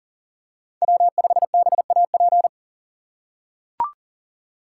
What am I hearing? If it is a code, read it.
Morse code, heard as a keyed tone, about 40 wpm, 700 Hz. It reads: W5BAP
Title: USA Call Signs - 5 Characters - 3x Character Spacing - 40wpm